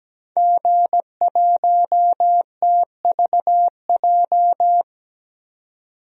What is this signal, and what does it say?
Morse code, heard as a keyed tone, about 17 wpm, 700 Hz. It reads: G1TVJ
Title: DXCC-100 Call Signs - 5 Characters - 17wpm